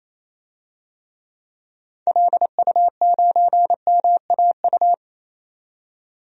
Morse code, heard as a keyed tone, about 28 wpm, 700 Hz. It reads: LU9MAU
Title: DXCC-100 Call Signs - 6 Characters - 28wpm